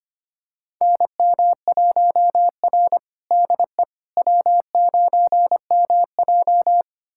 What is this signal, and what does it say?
Morse code, heard as a keyed tone, about 25 wpm, 700 Hz. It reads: NM1R DE W9MJ
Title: USA Callsign DE Callsign - 4 Characters - 25wpm